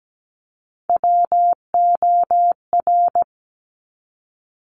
Morse code, heard as a keyed tone, about 17 wpm, 700 Hz. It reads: WOR
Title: Top Three Letter Combinations in Words 17wpm